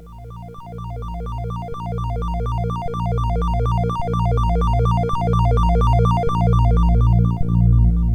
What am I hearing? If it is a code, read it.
There is no code.